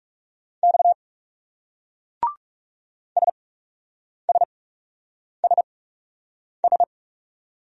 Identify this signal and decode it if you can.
Morse code, heard as a keyed tone, about 36 wpm, spaced as for 8 wpm, 700 Hz. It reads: X SSSS
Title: Speed-Racing Single Letters 30wpm